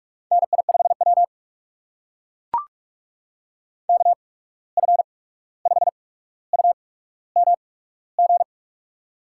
Morse code, heard as a keyed tone, about 45 wpm, spaced as for 17 wpm, 700 Hz. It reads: NI5W KF5UMG
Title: USA Call Signs - 8x Character Spacing - 45wpm